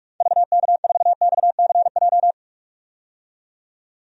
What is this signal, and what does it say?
Morse code, heard as a keyed tone, about 45 wpm, 700 Hz. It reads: VK4XCJ
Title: DXCC-100 Call Signs - 6 Characters - 45wpm